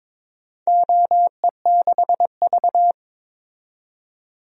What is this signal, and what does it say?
Morse code, heard as a keyed tone, about 22 wpm, 700 Hz. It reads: OE6V
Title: DXCC-100 Call Signs - 4 Characters - 22wpm